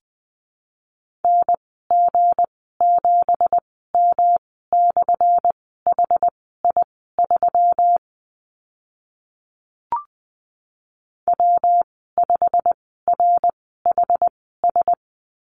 Morse code, heard as a keyed tone, about 20 wpm, spaced as for 14 wpm, 700 Hz. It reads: NG7M/HI3 W5RHS
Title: 186 Lesson 30g - Letter b - Call Signs 20wpm